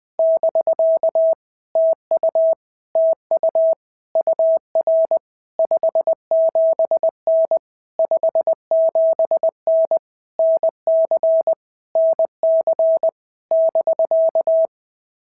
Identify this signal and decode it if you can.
Morse code, heard as a keyed tone, about 20 wpm, 650 Hz. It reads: <BK> TU TU UR 57N 57N NC NC <BK>
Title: POTA QSO Practice - 20wpm